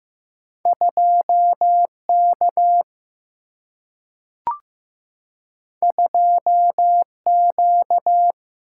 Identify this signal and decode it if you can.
Morse code, heard as a keyed tone, about 15 wpm, 700 Hz. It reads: 2K 2Q
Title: Single-Letter Single-Number 15wpm